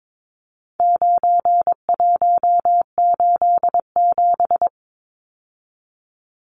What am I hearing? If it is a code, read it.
Morse code, heard as a keyed tone, about 22 wpm, 700 Hz. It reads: 9187
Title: Numbers - 4-Digit 22wpm